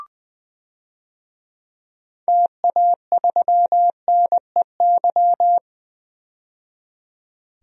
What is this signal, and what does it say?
Morse code, heard as a keyed tone, about 21 wpm, 700 Hz. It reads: TA3NEY
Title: DXCC-100 Call Signs - 6 Characters - 20wpm